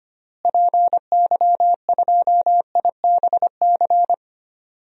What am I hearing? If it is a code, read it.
Morse code, heard as a keyed tone, about 25 wpm, 700 Hz. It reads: PY2IBC
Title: DXCC-100 Call Signs - 6 Characters - Repeated 3x - 25wpm